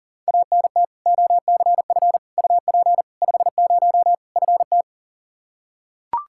Morse code, heard as a keyed tone, about 40 wpm, 700 Hz. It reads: ANT OCF UP 50 FT
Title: Practice QSO - Exchange 2 - Info - 40wpm